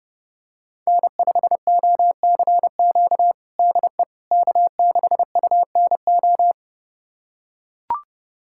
Morse code, heard as a keyed tone, about 30 wpm, 700 Hz. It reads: N5OCQ DE K6UNO
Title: USA Callsign DE Callsign - 5 Characters - 30wpm